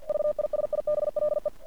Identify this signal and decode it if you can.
Morse code, heard as a keyed tone, about 48 wpm, 600 Hz. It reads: VISIBLE
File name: archived / CRSnet